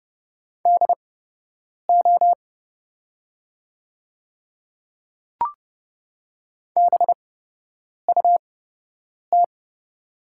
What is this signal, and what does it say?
Morse code, heard as a keyed tone, about 30 wpm, spaced as for 8 wpm, 700 Hz. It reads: DO BUT
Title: 8x Character Spacing - Top 100 Words - 30wpm